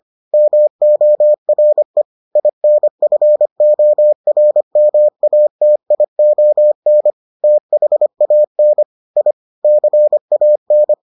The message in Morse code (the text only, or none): MORE INFORMATION THAN I CAN